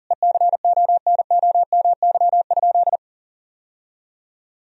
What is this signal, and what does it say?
Morse code, heard as a keyed tone, about 40 wpm, 700 Hz. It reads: ECONOMY?
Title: Sentences from Top 1000 Words - Review Part 3 - 40wpm